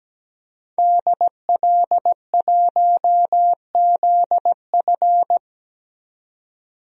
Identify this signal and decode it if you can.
Morse code, heard as a keyed tone, about 17 wpm, 700 Hz. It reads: DL1ZF
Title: DXCC-100 Call Signs - 5 Characters - 17wpm